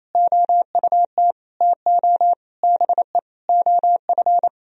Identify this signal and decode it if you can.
Morse code, heard as a keyed tone, about 28 wpm, 700 Hz. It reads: OUT TO BE OF
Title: Sentences from Top 400 Words and Abbreviations 28wpm